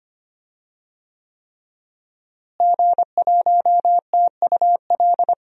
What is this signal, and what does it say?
Morse code, heard as a keyed tone, about 25 wpm, 700 Hz. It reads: G1TUL
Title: DXCC-100 Call Signs - 5 Characters - Repeated 3x - 25wpm